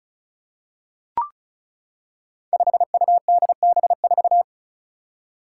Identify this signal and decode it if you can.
Morse code, heard as a keyed tone, about 36 wpm, 700 Hz. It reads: 5UDB4
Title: Five Challenging Letter-Number Combinations 35wpm